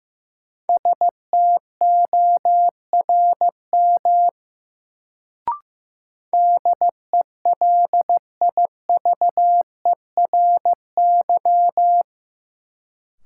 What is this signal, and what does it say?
Morse code, heard as a keyed tone, about 15 wpm, 700 Hz. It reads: STORM DELIVERY
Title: Sentences from Top 1500 Words - New Word Drill - 15wpm